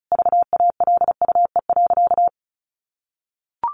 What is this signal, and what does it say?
Morse code, heard as a keyed tone, about 34 wpm, 700 Hz. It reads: VALUE.
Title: Sentences from Top 1300 Words 35wpm